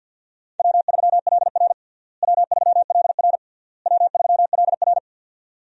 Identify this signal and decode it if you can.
Morse code, heard as a keyed tone, about 52 wpm, 700 Hz. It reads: W3LR W3LR W3LR
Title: USA Call Signs - 4 Characters - Repeated 3x - 50wpm